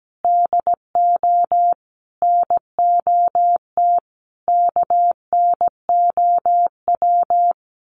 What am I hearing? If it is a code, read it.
Morse code, heard as a keyed tone, about 17 wpm, 700 Hz. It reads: DO NOT KNOW
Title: Sets of 5 Words Encore2 17wpm